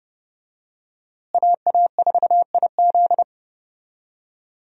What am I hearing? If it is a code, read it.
Morse code, heard as a keyed tone, about 32 wpm, 700 Hz. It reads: AA4IZ
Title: USA Call Signs - 5 Characters - Repeated 3x - 30wpm